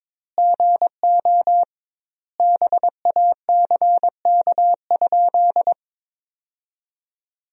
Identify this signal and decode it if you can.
Morse code, heard as a keyed tone, about 22 wpm, 700 Hz. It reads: GO BACK?
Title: 2x Word-Spacing - Sentences from Top 400 Words - 22wpm